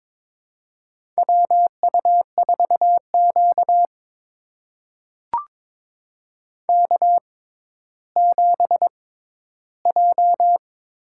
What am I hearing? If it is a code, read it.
Morse code, heard as a keyed tone, about 23 wpm, spaced as for 13 wpm, 700 Hz. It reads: WU4Q K7J
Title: USA Call Signs - 4 Characters - 6x Character Spacing - 22wpm